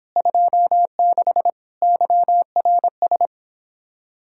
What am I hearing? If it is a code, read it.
Morse code, heard as a keyed tone, about 31 wpm, 700 Hz. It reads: 26 YRS